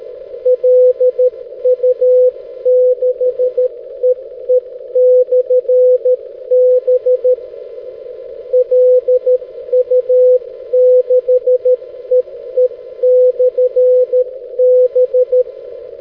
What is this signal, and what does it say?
Morse code, heard as a keyed tone, about 13 wpm, 500 Hz. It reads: LU6EE/B LU6EE/B
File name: 6m-lu6ee-b